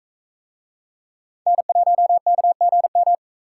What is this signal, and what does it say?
Morse code, heard as a keyed tone, about 43 wpm, 700 Hz. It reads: N1KGM